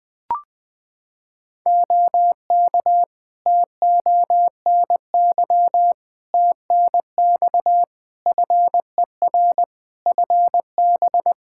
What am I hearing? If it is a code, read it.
Morse code, heard as a keyed tone, about 20 wpm, 700 Hz. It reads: OK TONY TNX FER FB
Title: Practice QSO - Exchange 4 - Ending - 20wpm